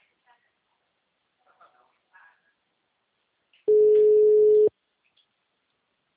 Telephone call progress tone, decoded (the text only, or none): ringback tone